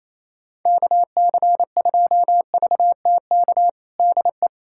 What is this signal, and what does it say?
Morse code, heard as a keyed tone, about 28 wpm, 700 Hz. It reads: KC2VTK DE
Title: USA Callsign DE Callsign 28wpm